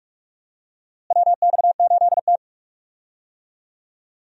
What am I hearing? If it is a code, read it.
Morse code, heard as a keyed tone, about 46 wpm, 700 Hz. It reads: WX8T
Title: USA Call Signs - 4 Characters - Repeated 3x - 45wpm